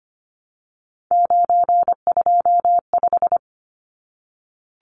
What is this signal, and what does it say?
Morse code, heard as a keyed tone, about 25 wpm, 690 Hz. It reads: 925
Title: Numbers - 3-Digit 25wpm